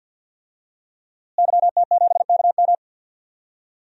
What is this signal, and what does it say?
Morse code, heard as a keyed tone, about 47 wpm, 700 Hz. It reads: YT7KM
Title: DXCC-100 Call Signs - 5 Characters - Repeated 2x - 50wpm